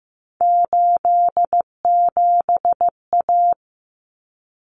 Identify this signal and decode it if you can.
Morse code, heard as a keyed tone, about 15 wpm, 700 Hz. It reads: 87A
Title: Common Cut Numbers - 3-Digit 15wpm